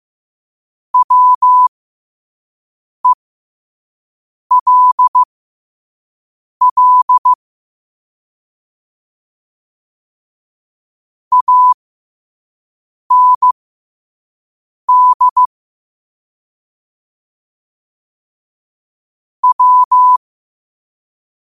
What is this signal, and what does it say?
Morse code, heard as a keyed tone, about 15 wpm, spaced as for 5 wpm, 1000 Hz. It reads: WELL AND W